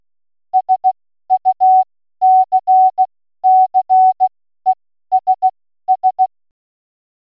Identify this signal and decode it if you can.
Morse code, heard as a keyed tone, about 16 wpm, spaced as for 12 wpm, 750 Hz. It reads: SUCCESS